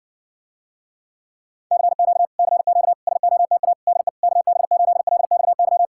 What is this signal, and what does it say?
Morse code, heard as a keyed tone, about 60 wpm, 700 Hz. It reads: CQ CQ SOTA DE KB8LXY